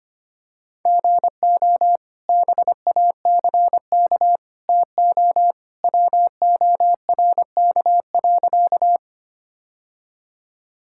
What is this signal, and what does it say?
Morse code, heard as a keyed tone, about 25 wpm, 700 Hz. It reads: GO BACK TO WORK.
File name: practice / 25wpm / Sentences with 2-Word Phrases - 25wpm